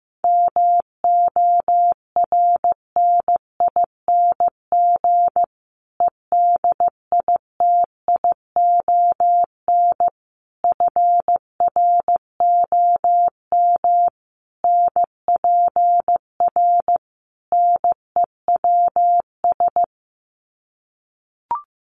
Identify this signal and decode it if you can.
Morse code, heard as a keyed tone, about 15 wpm, 700 Hz. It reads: MORNING EDITION FROM NPR NEWS
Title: Sets of 5 Words Encore 15wpm